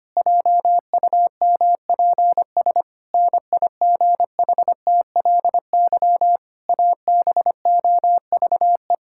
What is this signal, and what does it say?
Morse code, heard as a keyed tone, about 25 wpm, 700 Hz. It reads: JUMPS NIGHTLY ABOVE
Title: Pangram Challenge v2 - 25wpm